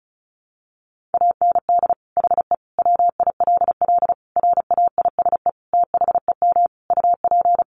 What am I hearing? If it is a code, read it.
Morse code, heard as a keyed tone, about 34 wpm, 700 Hz. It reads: AND HE WILL RAISE THEM UP
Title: Sentences from Top 1400 Words 35wpm